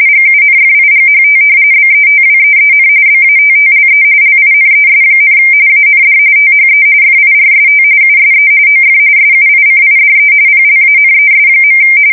RTTY, about 45.45 baud. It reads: GUNFIGHT BETWEEN BATTLE LINES, AND THE FIRST USE OF SUICIDE PLANES.
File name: fsk170 / battleshipnj